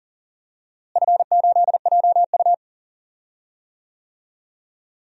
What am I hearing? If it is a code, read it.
Morse code, heard as a keyed tone, about 40 wpm, 700 Hz. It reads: F8JU
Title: DXCC-100 Call Signs - 4 Characters - Repeated 2x - 40wpm